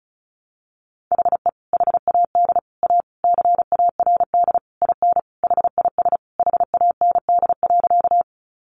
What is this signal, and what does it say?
Morse code, heard as a keyed tone, about 34 wpm, 700 Hz. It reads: HE HAD A CARD IN HIS HAND.
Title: Sentences from Top 600 Words 35wpm